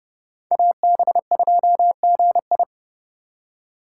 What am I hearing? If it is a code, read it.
Morse code, heard as a keyed tone, about 30 wpm, 700 Hz. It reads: AB2GI
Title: K1USN SST QSO Practice - 30wpm